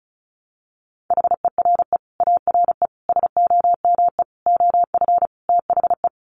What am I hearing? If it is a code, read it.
Morse code, heard as a keyed tone, about 34 wpm, 700 Hz. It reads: HERE ARE SOME OF THE
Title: Sets of 5 Words Encore 35wpm